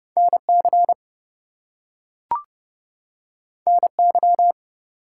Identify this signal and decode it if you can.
Morse code, heard as a keyed tone, about 30 wpm, 700 Hz. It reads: NC NY
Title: US States and Canadian Provinces 30wpm